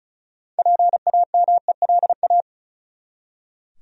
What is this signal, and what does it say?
Morse code, heard as a keyed tone, about 36 wpm, 700 Hz. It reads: PAMELA
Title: USA Common Names - 35wpm